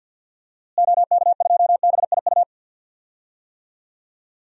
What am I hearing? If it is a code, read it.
Morse code, heard as a keyed tone, about 48 wpm, 700 Hz. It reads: OK2BIU
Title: DXCC-100 Call Signs - 6 Characters - Repeated 2x - 50wpm